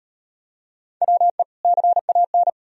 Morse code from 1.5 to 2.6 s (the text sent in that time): CAN